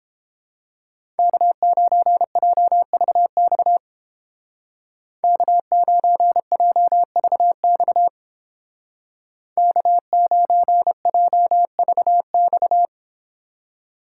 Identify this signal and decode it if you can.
Morse code, heard as a keyed tone, about 29 wpm, 700 Hz. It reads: K9JVX K9JVX K9JVX